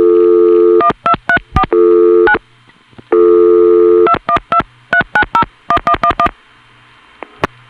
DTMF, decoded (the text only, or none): 1234 9 2123901111